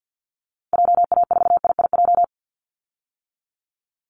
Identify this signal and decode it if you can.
Morse code, heard as a keyed tone, about 50 wpm, 700 Hz. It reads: JA4IIJ
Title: Call Signs - 50wpm